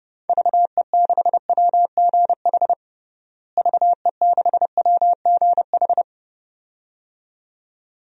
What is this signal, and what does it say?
Morse code, heard as a keyed tone, about 30 wpm, 700 Hz. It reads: VE6WGH VE6WGH
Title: DXCC-100 Call Signs - 6 Characters - Repeated 2x - 30wpm